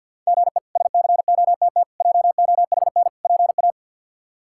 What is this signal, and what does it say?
Morse code, heard as a keyed tone, about 52 wpm, 700 Hz. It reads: GE SCOTT JOHN PA
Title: K1USN SST QSO Practice - 50wpm